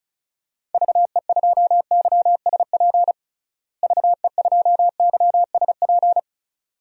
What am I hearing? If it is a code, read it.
Morse code, heard as a keyed tone, about 36 wpm, 700 Hz. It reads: VE2YSP VE2YSP
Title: DXCC-100 Call Signs - 6 Characters - Repeated 2x - 35wpm